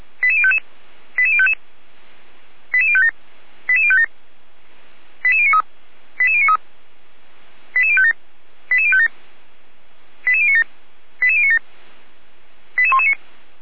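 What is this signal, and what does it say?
Five-tone selective call (ZVEI), heard as quick runs of five tones, tones of about 70 ms. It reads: 88055 88055 88057 88057 88093 88093 88057 88057 88097 88097 88119
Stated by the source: ZVEI five-tone selective call 88053 received on SDR.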